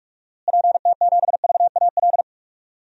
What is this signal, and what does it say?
Morse code, heard as a keyed tone, about 46 wpm, 700 Hz. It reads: PT7VAL
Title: DXCC-100 Call Signs - 6 Characters - Repeated 3x - 45wpm